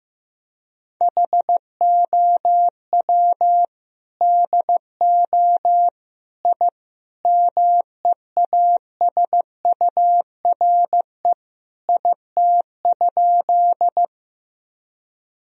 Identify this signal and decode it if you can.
Morse code, heard as a keyed tone, about 15 wpm, 700 Hz. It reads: HOW DO I MEASURE IT?
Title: Sentences from Top 1000 Words - Review Part 2 - 15wpm